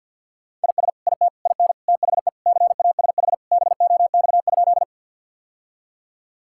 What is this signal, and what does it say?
Morse code, heard as a keyed tone, about 52 wpm, 700 Hz. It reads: IS IT IN THE CASH BOX?
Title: Sentences from Top 1000 Words - Review Part 5 - 50wpm